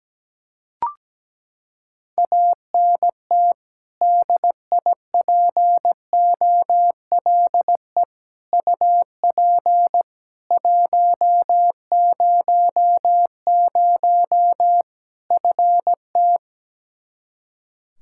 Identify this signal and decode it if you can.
Morse code, heard as a keyed tone, about 17 wpm, 700 Hz. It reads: ANT DIPOLE UP 100 FT
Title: Practice QSO - Exchange 2 - Info - 17wpm